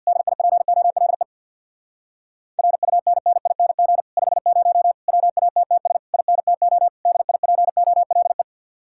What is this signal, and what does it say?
Morse code, heard as a keyed tone, about 63 wpm, 700 Hz. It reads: DIPOLE RUNNING 50 WATTS INTO DIPOLE